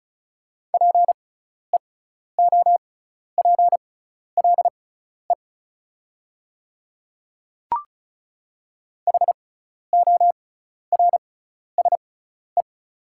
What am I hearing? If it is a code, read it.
Morse code, heard as a keyed tone, about 36 wpm, spaced as for 12 wpm, 690 Hz. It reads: PEOPLE HORSE
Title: 6x Character Spacing - Top 300 Words - 35wpm